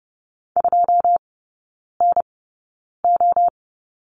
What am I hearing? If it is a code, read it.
Morse code, heard as a keyed tone, about 30 wpm, spaced as for 9 wpm, 700 Hz. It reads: 2NO
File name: 081 Lesson 14e - Number 2 - Three Characters 30wpm